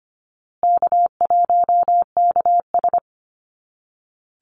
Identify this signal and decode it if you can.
Morse code, heard as a keyed tone, about 25 wpm, 700 Hz. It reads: K1KS
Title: Easy Call Signs Repeated 3x - 25wpm